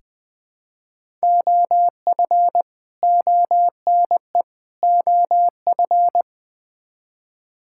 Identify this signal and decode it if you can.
Morse code, heard as a keyed tone, about 20 wpm, 700 Hz. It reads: OF ONE OF
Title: Build A Phrase - 4-Word Phrases - 20wpm